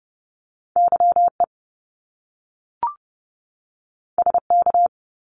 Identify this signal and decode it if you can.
Morse code, heard as a keyed tone, about 30 wpm, 700 Hz. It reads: YE SK
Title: Top Two Letter Combinations in Words 30wpm